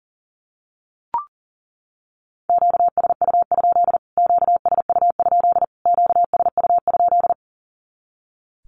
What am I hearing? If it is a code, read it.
Morse code, heard as a keyed tone, about 40 wpm, 700 Hz. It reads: QSU? QSU? QSU?
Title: Q-Codes Repeated 3x 40wpm